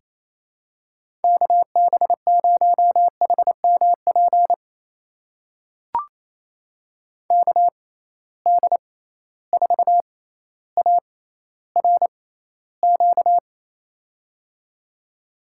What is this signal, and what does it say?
Morse code, heard as a keyed tone, about 28 wpm, spaced as for 14 wpm, 700 Hz. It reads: KB0HMP KD4ARQ
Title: USA Call Signs - 6 Characters - 6x Character Spacing - 28wpm